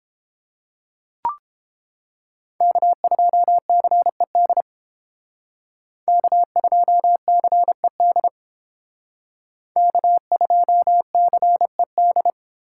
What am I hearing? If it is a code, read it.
Morse code, heard as a keyed tone, about 31 wpm, 700 Hz. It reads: K2CED K2CED K2CED